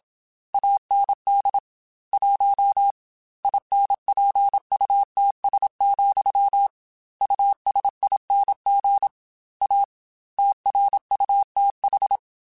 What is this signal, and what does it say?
Morse code, heard as a keyed tone, about 27 wpm, 800 Hz. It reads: AND 1 INPUTS, USING A TRUTH
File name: Technical Subjects / Technical Subjects-mp3s / Gates Intro @25